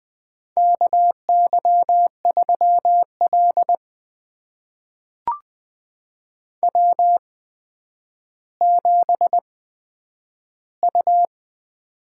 Morse code, heard as a keyed tone, about 20 wpm, spaced as for 10 wpm, 700 Hz. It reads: KY3L W7U
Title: USA Call Signs - 4 Characters - 8x Character Spacing - 20wpm